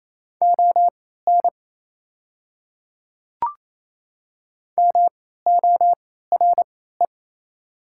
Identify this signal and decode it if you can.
Morse code, heard as a keyed tone, about 27 wpm, spaced as for 16 wpm, 690 Hz. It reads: ON MORE
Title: 3x Character Spacing - Top 100 Words - 28wpm